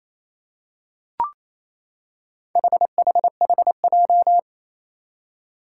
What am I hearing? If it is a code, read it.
Morse code, heard as a keyed tone, about 28 wpm, 700 Hz. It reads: HHHJ
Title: Four Challenging Letter-Number Combinations 28wpm